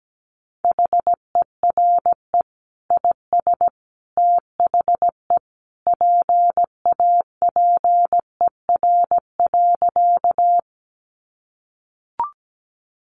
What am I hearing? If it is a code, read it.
Morse code, heard as a keyed tone, about 17 wpm, 700 Hz. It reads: HERE IS THE PAPER.
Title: Sentences from Top 300 Words 17wpm